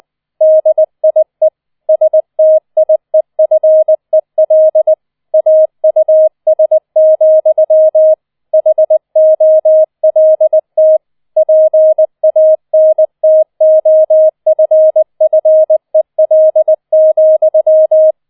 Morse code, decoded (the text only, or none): DIE STIEFEL AUS, HOLT PANTOFFEL,